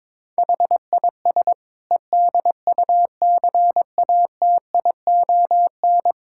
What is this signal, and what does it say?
Morse code, heard as a keyed tone, about 22 wpm, 700 Hz. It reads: HIS EDUCATION
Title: Sentences from Top 1000 Words - Review Part 6 - 22wpm